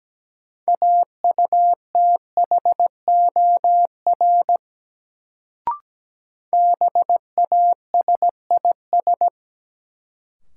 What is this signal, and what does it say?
Morse code, heard as a keyed tone, about 17 wpm, 700 Hz. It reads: AUTHOR BASIS
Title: Sentences from Top 1200 Words - New Word Drill Alt - 17wpm